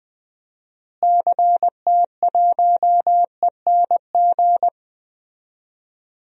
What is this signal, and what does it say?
Morse code, heard as a keyed tone, about 20 wpm, 700 Hz. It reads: CT1ENG
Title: DXCC-100 Call Signs - 6 Characters - Repeated 2x - 20wpm